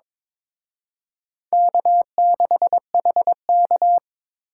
Morse code, heard as a keyed tone, about 26 wpm, 700 Hz. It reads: K6HK
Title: Speed Racing - USA Call Signs 22wpm